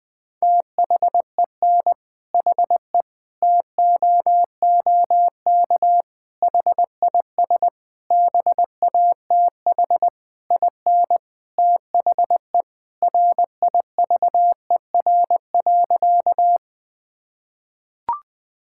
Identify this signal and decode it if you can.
Morse code, heard as a keyed tone, about 20 wpm, 700 Hz. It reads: THEN HE TOOK HIS BATH IN THE RIVER.
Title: Sentences from Top 1600 Words 20wpm